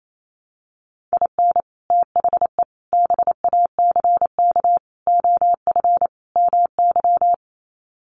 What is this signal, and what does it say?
Morse code, heard as a keyed tone, about 28 wpm, 700 Hz. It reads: IN THE BACK OF MY
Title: Sets of 5 Words Encore 28wpm